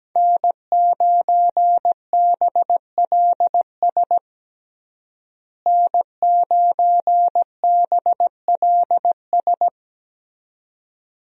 Morse code, heard as a keyed tone, about 17 wpm, 700 Hz. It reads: N9BLS N9BLS
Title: USA Call Signs - 5 Characters - Repeated 2x - 17wpm